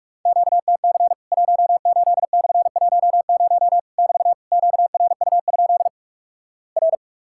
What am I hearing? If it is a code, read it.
Morse code, heard as a keyed tone, about 46 wpm, 700 Hz. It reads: QTC 18/10 <BT> QRU? R
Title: Worked All Europe DX Contest - QTC Exchange - 45wpm